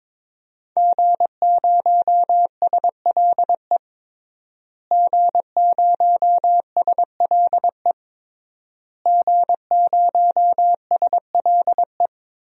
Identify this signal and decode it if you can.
Morse code, heard as a keyed tone, about 22 wpm, 700 Hz. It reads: G0SLE G0SLE G0SLE
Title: DXCC-100 Call Signs - 5 Characters - Repeated 3x - 22wpm